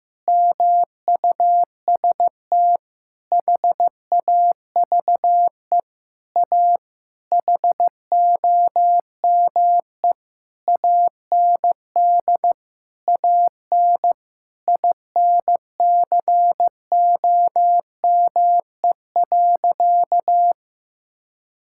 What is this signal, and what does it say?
Morse code, heard as a keyed tone, about 15 wpm, 700 Hz. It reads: MUST HAVE A HOME AND AN INCOME.
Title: Sentences from Top 1000 Words - Review Part 5 - 15wpm